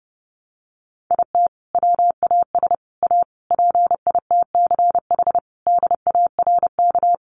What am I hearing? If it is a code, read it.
Morse code, heard as a keyed tone, about 30 wpm, 700 Hz. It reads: IT WAS A PITCH DARK
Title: Sentences from Top 1300 Words 30wpm